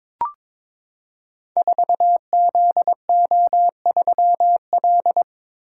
Morse code, heard as a keyed tone, about 22 wpm, 700 Hz. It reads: 4ZO3L
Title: Code Groups - 5 Letter:Numbers - 22wpm